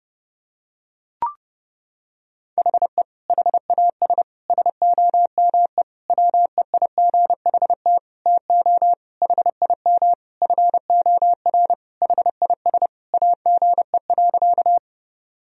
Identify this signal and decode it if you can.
Morse code, heard as a keyed tone, about 32 wpm, 700 Hz. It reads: HE HAS SOME WEIGHT TO HIM FOR HIS AGE.
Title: Sentences from Top 1000 Words - Review Part 3 - 30wpm